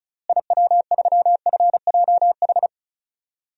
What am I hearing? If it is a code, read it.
Morse code, heard as a keyed tone, about 36 wpm, 700 Hz. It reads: IW3FJH
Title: DXCC-100 Call Signs - 6 Characters - Repeated 3x - 35wpm